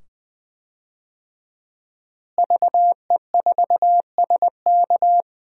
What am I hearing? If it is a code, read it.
Morse code, heard as a keyed tone, about 20 wpm, 700 Hz. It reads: VE4SK
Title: DXCC-100 Call Signs - 5 Characters - Repeated 2x - 20wpm